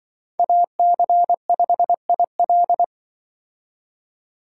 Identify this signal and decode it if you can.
Morse code, heard as a keyed tone, about 25 wpm, 700 Hz. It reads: AC5IL
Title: Speed Racing - USA 2x2 Call Signs 20wpm